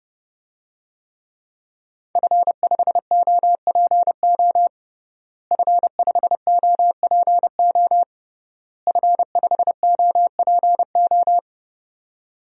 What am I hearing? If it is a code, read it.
Morse code, heard as a keyed tone, about 30 wpm, 700 Hz. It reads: F5OPO F5OPO F5OPO
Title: DXCC-100 Call Signs - 5 Characters - Repeated 3x - 30wpm